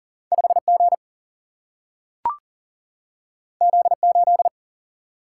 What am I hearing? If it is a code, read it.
Morse code, heard as a keyed tone, about 41 wpm, 700 Hz. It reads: 5G Z8
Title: Single-Letter Single-Number 40wpm